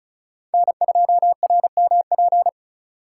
Morse code, heard as a keyed tone, about 36 wpm, 700 Hz. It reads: N2RMP
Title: USA Call Signs - 5 Characters - 3x Character Spacing - 35wpm